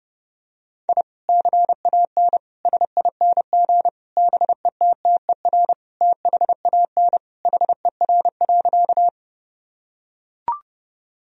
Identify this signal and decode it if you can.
Morse code, heard as a keyed tone, about 32 wpm, 700 Hz. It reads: I CAN SING BETTER THAN HER.
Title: Sentences from Top 1000 Words - Review Part 2 - 30wpm